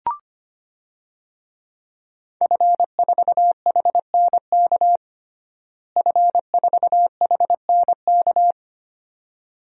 Morse code, heard as a keyed tone, about 25 wpm, 700 Hz. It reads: F4HNK F4HNK
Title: DXCC-100 Call Signs - 5 Characters - Repeated 2x - 25wpm